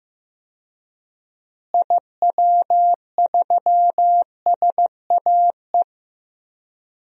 Morse code, heard as a keyed tone, about 15 wpm, 700 Hz. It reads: IW3SAE
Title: DXCC-100 Call Signs - 6 Characters - Repeated 3x - 15wpm